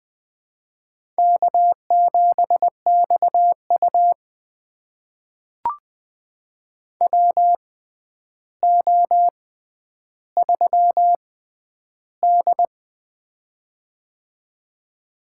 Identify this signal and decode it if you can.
Morse code, heard as a keyed tone, about 20 wpm, spaced as for 10 wpm, 700 Hz. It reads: K7XU WO3D
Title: USA Call Signs - 6x Character Spacing - 20wpm